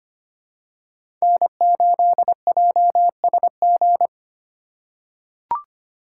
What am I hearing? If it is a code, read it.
Morse code, heard as a keyed tone, about 25 wpm, 700 Hz. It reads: N8JSG
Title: USA Call Signs - 5 Characters - 3x Character Spacing - 25wpm